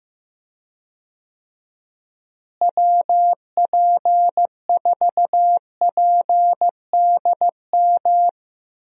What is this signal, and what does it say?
Morse code, heard as a keyed tone, about 15 wpm, 700 Hz. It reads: WP4PDM
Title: USA Call Signs - 6 Characters - Repeated 3x - 15wpm